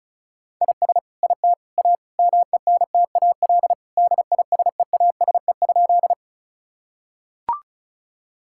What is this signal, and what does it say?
Morse code, heard as a keyed tone, about 36 wpm, 700 Hz. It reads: IS IT A MENTAL DISEASE?
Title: Sentences from Top 1000 Words - Review Part 6 - 35wpm